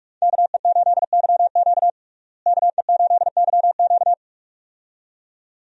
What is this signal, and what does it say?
Morse code, heard as a keyed tone, about 45 wpm, 700 Hz. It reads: KE8YQ KE8YQ
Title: USA Call Signs - 5 Characters - Repeated 2x - 45wpm